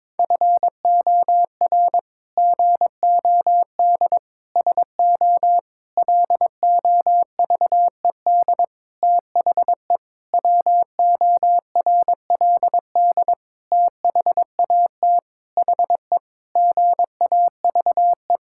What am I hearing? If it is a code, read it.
Morse code, heard as a keyed tone, about 22 wpm, 700 Hz. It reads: FOR GOD SO LOVED THE WORLD THAT HE GAVE
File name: practice / 22wpm / Sentences with 8-Word Phrases - 22wpm